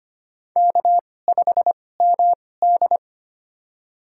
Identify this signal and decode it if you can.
Morse code, heard as a keyed tone, about 25 wpm, spaced as for 18 wpm, 700 Hz. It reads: K5MD
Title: USA Call Signs - 4 Characters - 2x Character Spacing - 25wpm